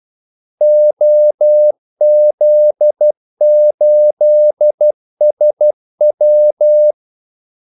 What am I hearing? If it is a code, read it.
Morse code, heard as a keyed tone, about 12 wpm, 600 Hz. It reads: OZ8SW
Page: 2 ekstra ordmellemrum mellem kaldesignalerne